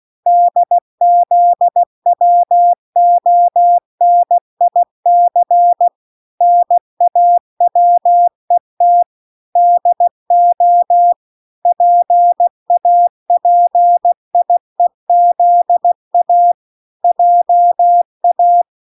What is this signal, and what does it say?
Morse code, heard as a keyed tone, about 16 wpm, 700 Hz. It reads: DZWONIC NAWET DO PAPIEZA JA